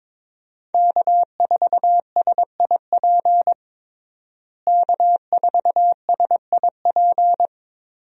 Morse code, heard as a keyed tone, about 22 wpm, 700 Hz. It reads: K4SIP K4SIP
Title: USA Call Signs - 5 Characters - Repeated 2x - 22wpm